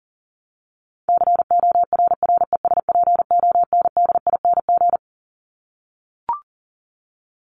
Morse code, heard as a keyed tone, about 40 wpm, 700 Hz. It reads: CORRESPONDING
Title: ING Suffix - Top 100 Words with Base Words 40wpm